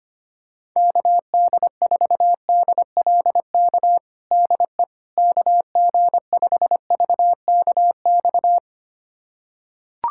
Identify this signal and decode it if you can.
Morse code, heard as a keyed tone, about 25 wpm, 700 Hz. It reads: KD4DLK DE KG5VKX
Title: USA Callsign DE Callsign - 6 Characters - 25wpm